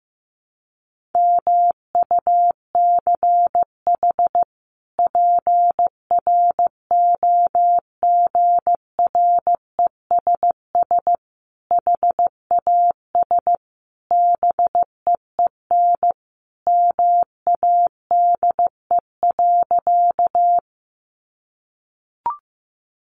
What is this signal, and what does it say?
Morse code, heard as a keyed tone, about 15 wpm, 700 Hz. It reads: MUCH PROGRESS HAS BEEN MADE.
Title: Sentences from Top 1400 Words 15wpm